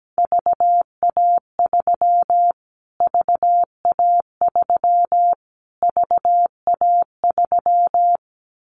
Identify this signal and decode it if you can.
Morse code, heard as a keyed tone, about 17 wpm, 700 Hz. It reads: VA3 VA3 VA3
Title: Callsign Prefixes for Top 25 DXCC - Repeated 3x - 17wpm